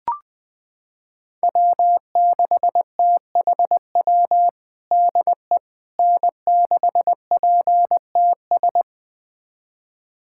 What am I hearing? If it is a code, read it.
Morse code, heard as a keyed tone, about 20 wpm, 700 Hz. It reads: W6THW DE N6PTS
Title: USA Callsign DE Callsign - 5 Characters - 20wpm